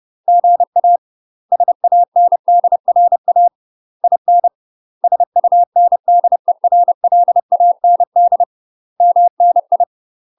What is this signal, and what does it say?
Morse code, heard as a keyed tone, about 30 wpm, 700 Hz. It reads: GA SANDRA IN SUNDERLAND MNI
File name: UK MASTER QSOs TRAINING / 30WPM / UK MASTER QSOs 0046